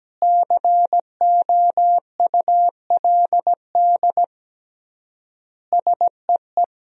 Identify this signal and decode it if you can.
Morse code, heard as a keyed tone, about 17 wpm, 700 Hz. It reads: COULD SEE
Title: 3x Word-Spacing - Sentences from Top 500 Words - 17wpm